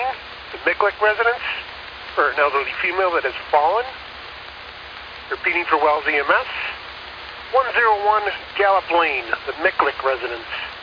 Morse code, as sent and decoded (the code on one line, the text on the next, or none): none